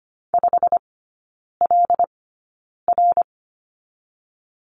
Morse code, heard as a keyed tone, about 25 wpm, spaced as for 9 wpm, 700 Hz. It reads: 5LR
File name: Lesson 15e - Number 5 - Three Characters 25wpm